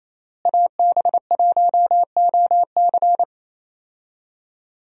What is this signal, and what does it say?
Morse code, heard as a keyed tone, about 28 wpm, 700 Hz. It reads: AB1OC
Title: USA Call Signs - 5 Characters - Repeated 3x - 28wpm